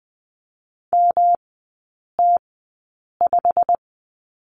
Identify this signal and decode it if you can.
Morse code, heard as a keyed tone, about 20 wpm, spaced as for 8 wpm, 700 Hz. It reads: MT5
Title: 107 Lesson 18e - Letter m - Three Characters 20wpm